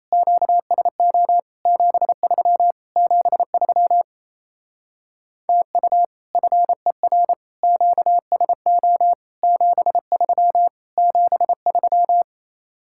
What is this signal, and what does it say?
Morse code, heard as a keyed tone, about 36 wpm, 700 Hz. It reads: QSO 73 73 TU FER QSO 73 73